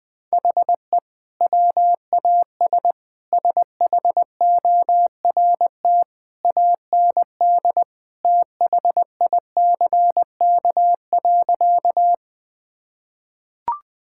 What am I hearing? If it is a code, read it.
Morse code, heard as a keyed tone, about 21 wpm, 700 Hz. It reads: HE WAS SHORT AND THICK.
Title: Sentences from Top 1600 Words 20wpm